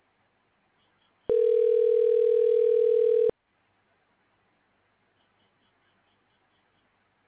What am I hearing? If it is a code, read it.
Ringback tone (North America), 440 + 480 Hz.